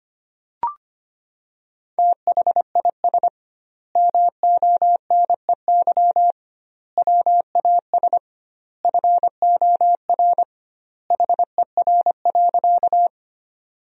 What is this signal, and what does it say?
Morse code, heard as a keyed tone, about 25 wpm, 700 Hz. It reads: THIS MONEY WAS FOR HER.
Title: 2x Word-Spacing - Sentences from Top 400 Words - 25wpm